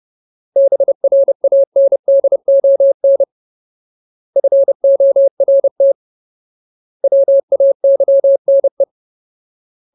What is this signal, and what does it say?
Morse code, heard as a keyed tone, about 30 wpm, 550 Hz. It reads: BRANDON FORT WAYNE
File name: Ultimate Word List / 30WPM Ultimate